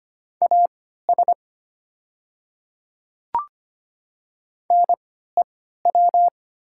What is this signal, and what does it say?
Morse code, heard as a keyed tone, about 25 wpm, spaced as for 14 wpm, 700 Hz. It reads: AS NEW
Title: 3x Character Spacing - Top 300 Words - 25wpm